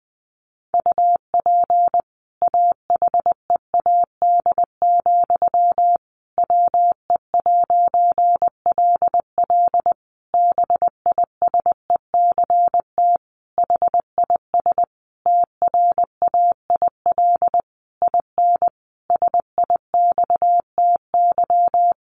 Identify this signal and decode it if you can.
Morse code, heard as a keyed tone, about 20 wpm, 700 Hz. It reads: UP AHEAD, WE'LL BISECT HIS TRAIL IN SIXTY
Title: A Sound Of Thunder 20wpm